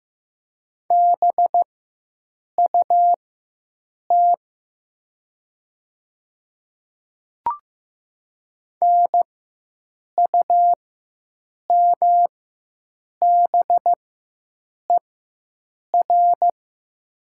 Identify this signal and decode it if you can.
Morse code, heard as a keyed tone, about 15 wpm, spaced as for 7 wpm, 700 Hz. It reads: BUT NUMBER
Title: 4x Character Spacing - Top 200 Words - 15wpm